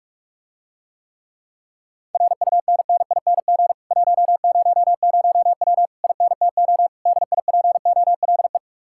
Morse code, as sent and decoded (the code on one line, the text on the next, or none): .-. ..- -. -. .. -. --. / .---- ----- ----- .-- / .. -. - --- / -.. .. .--. --- .-.. .
RUNNING 100W INTO DIPOLE